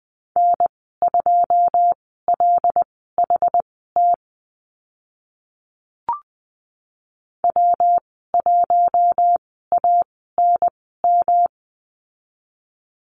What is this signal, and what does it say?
Morse code, heard as a keyed tone, about 20 wpm, spaced as for 14 wpm, 700 Hz. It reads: N2LHT W1ANM
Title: 128 Lesson 21g - Number 6 - Call Signs 20wpm